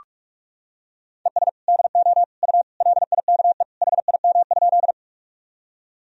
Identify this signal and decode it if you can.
Morse code, heard as a keyed tone, about 41 wpm, 700 Hz. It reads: ES DO U LIKE HIM?
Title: Sentences from Top 100 Words and Abbreviations 45wpm